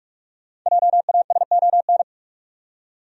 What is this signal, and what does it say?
Morse code, heard as a keyed tone, about 46 wpm, 700 Hz. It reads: JASON
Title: USA Common Names - 45wpm